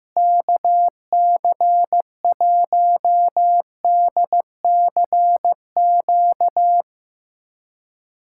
Morse code, heard as a keyed tone, about 15 wpm, 700 Hz. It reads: KC1DCQ
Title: USA Callsign DE Callsign 15wpm